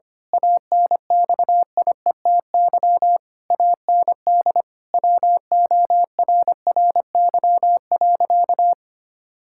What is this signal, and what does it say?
Morse code, heard as a keyed tone, about 25 wpm, 700 Hz. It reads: ANXIETY AND WORRY.
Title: Pangram Challenge v2 - 25wpm